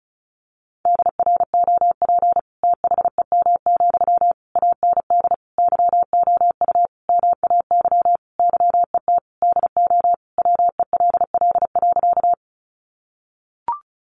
Morse code, heard as a keyed tone, about 34 wpm, 700 Hz. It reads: DROP THEM, AND YOU MAY YET DO WELL.
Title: Sentences from Top 1400 Words 35wpm